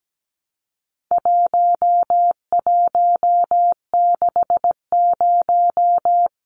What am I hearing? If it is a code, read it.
Morse code, heard as a keyed tone, about 17 wpm, 700 Hz. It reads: 1160
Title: Numbers - 4-Digit 17wpm